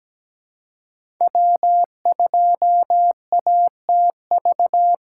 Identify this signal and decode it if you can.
Morse code, heard as a keyed tone, about 17 wpm, 700 Hz. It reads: W2ATV
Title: USA Call Signs - 3x Character Spacing - 17wpm